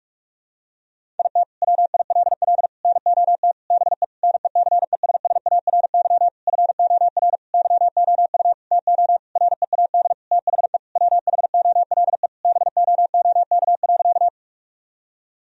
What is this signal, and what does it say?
Morse code, heard as a keyed tone, about 47 wpm, 700 Hz. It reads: IT WILL NOT BE NECESSARY FOR YOU TO READ THE WHOLE BOOK.
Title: Sentences from Top 1000 Words - Review Part 1 - 45wpm